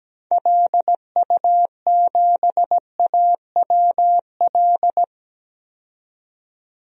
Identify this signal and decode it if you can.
Morse code, heard as a keyed tone, about 17 wpm, 700 Hz. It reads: LU7AWL
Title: DXCC-100 Call Signs - 6 Characters - 17wpm